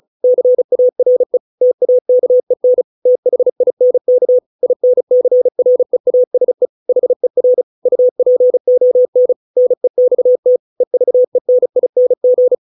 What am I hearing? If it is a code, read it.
Morse code, heard as a keyed tone, about 36 wpm, 500 Hz. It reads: CARE TAKEN THINK INCREASE HER UPON NEXT EVENING